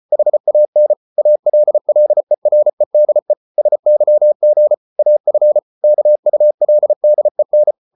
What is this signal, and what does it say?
Morse code, heard as a keyed tone, about 34 wpm, 600 Hz. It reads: HAN ALLEREDE SYG AF KULDEN
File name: Ferie QRQ 34wpm